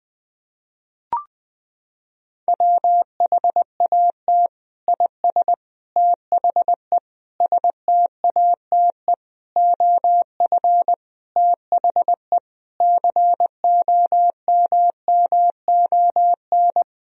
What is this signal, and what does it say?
Morse code, heard as a keyed tone, about 20 wpm, 700 Hz. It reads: WHAT IS THE STATE OF THE COMMON
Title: Sentences from Top 1000 Words - Review Part 3 - 20wpm